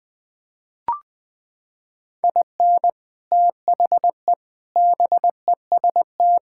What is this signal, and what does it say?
Morse code, heard as a keyed tone, about 20 wpm, 700 Hz. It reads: IN THE BEST
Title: Sets of 6 Words - 20wpm